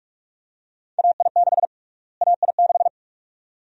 Morse code, heard as a keyed tone, about 45 wpm, 700 Hz. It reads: AI6 AI6
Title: USA Call Sign Prefixes - Repeated 2x - 45wpm